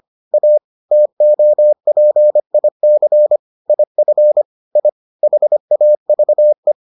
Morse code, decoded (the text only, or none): A TOPIC IF I HAVE